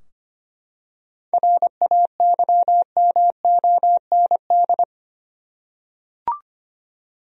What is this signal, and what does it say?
Morse code, heard as a keyed tone, about 25 wpm, 700 Hz. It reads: RAYMOND
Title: USA Common Names - 25wpm